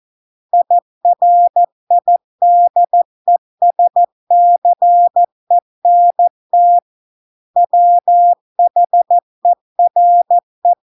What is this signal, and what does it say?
Morse code, heard as a keyed tone, about 14 wpm, 700 Hz. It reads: IRIDESCENT WHERE